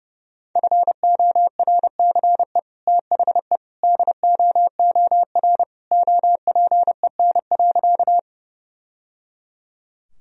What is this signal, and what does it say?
Morse code, heard as a keyed tone, about 32 wpm, 700 Hz. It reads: FORCE THE DOOR OPEN.
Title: Sentences from Top 1000 Words - Review Part 3 - 30wpm